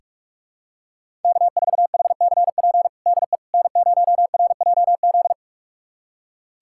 Morse code, heard as a keyed tone, about 46 wpm, 700 Hz. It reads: K4HCP DE N0RJZ
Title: USA Callsign DE Callsign - 5 Characters - 45wpm